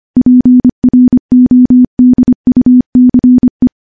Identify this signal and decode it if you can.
Morse code, heard as a keyed tone, about 25 wpm, 250 Hz. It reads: PRODUCE
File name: morse / morse-training-25WPM